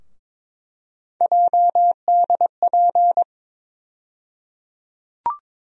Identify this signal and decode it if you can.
Morse code, heard as a keyed tone, about 22 wpm, 700 Hz. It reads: JDP
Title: USA Call Sign Postfixes - 3 Letters 22wpm